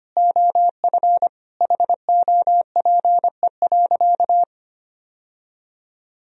Morse code, heard as a keyed tone, about 25 wpm, 700 Hz. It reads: OF HOPE.
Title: Sentences from Top 1000 Words - Review Part 3 - 25wpm